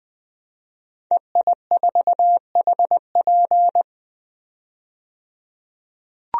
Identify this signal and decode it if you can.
Morse code, heard as a keyed tone, about 20 wpm, 700 Hz. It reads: EI4HP
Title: DXCC-100 Call Signs - 5 Characters - 20wpm